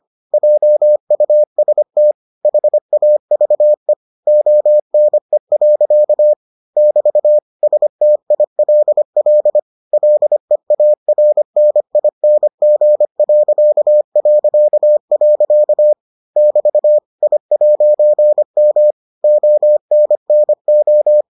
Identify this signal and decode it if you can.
Morse code, heard as a keyed tone, about 25 wpm, 600 Hz. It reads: JUST HAVE ONE. <BT> STILL LEARNING... <BT> I'M ONNO